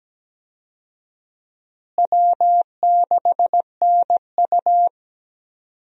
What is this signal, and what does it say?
Morse code, heard as a keyed tone, about 17 wpm, 700 Hz. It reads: W6NU
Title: USA Call Signs - 4 Characters - Repeated 3x - 17wpm